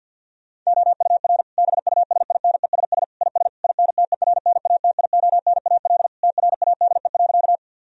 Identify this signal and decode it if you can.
Morse code, heard as a keyed tone, about 50 wpm, 700 Hz. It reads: OUR BUSINESS IS INTERNATIONAL TRADE.
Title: Sentences from Top 1000 Words - Review Part 4 - 50wpm